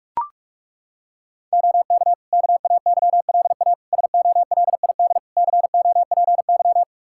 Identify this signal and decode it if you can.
Morse code, heard as a keyed tone, about 41 wpm, 700 Hz. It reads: OK KAYLA SOLID COPY
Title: K1USN SST QSO Practice 45wpm